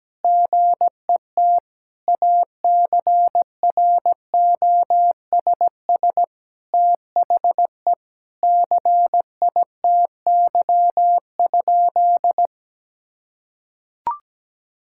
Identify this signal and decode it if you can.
Morse code, heard as a keyed tone, about 17 wpm, 700 Hz. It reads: GET ACROSS THE CITY?
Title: Sentences from Top 1000 Words - Review Part 5 - 17wpm